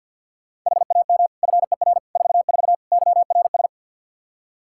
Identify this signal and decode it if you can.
Morse code, heard as a keyed tone, about 48 wpm, 700 Hz. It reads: HAM FER 44 YRS
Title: K1USN SST QSO Practice 50wpm